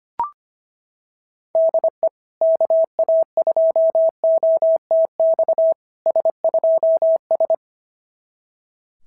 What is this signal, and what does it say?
Morse code, heard as a keyed tone, about 26 wpm, 650 Hz. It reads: DE KA2OTX S2S
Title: SOTA QSO Practice 25wpm